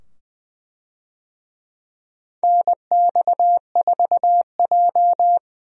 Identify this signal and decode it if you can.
Morse code, heard as a keyed tone, about 20 wpm, 700 Hz. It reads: NX4J
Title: USA Call Signs - 4 Characters - 20wpm